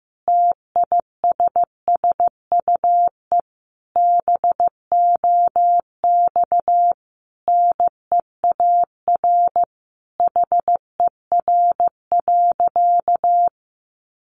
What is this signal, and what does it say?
Morse code, heard as a keyed tone, about 15 wpm, 700 Hz. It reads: TISSUE BOX NEAR HER.
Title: Easy Phrases 15wpm